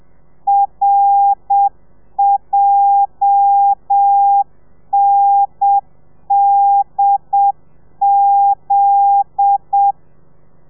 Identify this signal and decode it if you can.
Morse code, heard as a keyed tone, about 7 wpm, 800 Hz. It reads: RJNDZ